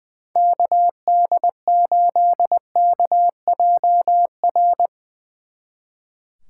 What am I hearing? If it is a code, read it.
Morse code, heard as a keyed tone, about 22 wpm, 700 Hz. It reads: KD8KJR
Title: Speed Racing - USA 2x3 Call Signs 20wpm